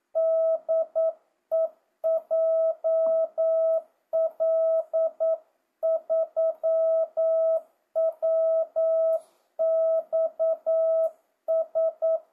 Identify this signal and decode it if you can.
Morse code, heard as a keyed tone, about 9 wpm, 650 Hz. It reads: DE JL3WXS
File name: jh8cbh / 2025cw